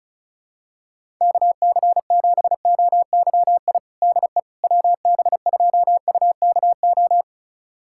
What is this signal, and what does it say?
Morse code, heard as a keyed tone, about 34 wpm, 700 Hz. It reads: KC7OYI DE WB2UKO
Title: USA Callsign DE Callsign - 6 Characters - 35wpm